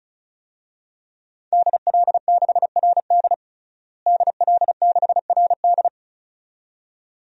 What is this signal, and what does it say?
Morse code, heard as a keyed tone, about 36 wpm, 700 Hz. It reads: DL6RD DL6RD
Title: DXCC-100 Call Signs - 5 Characters - Repeated 2x - 35wpm